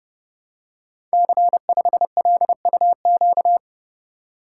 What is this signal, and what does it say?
Morse code, heard as a keyed tone, about 30 wpm, 700 Hz. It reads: C5LUQ
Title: Code Groups - 5 Letter:Numbers - 30wpm